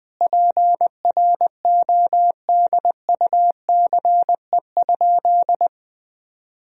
Morse code, heard as a keyed tone, about 20 wpm, 700 Hz. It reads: PRODUCE?
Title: Sentences from Top 1000 Words - Review Part 3 - 20wpm